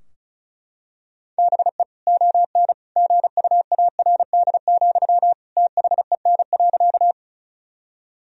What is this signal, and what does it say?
Morse code, heard as a keyed tone, about 36 wpm, 700 Hz. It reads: BE ON GUARD, THEN.
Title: Sentences from Top 1000 Words - Review Part 4 - 35wpm